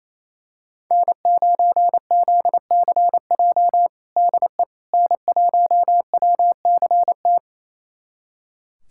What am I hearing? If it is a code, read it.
Morse code, heard as a keyed tone, about 27 wpm, 700 Hz. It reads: N9ZCJ DE N1WCT
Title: USA Callsign DE Callsign - 5 Characters - 28wpm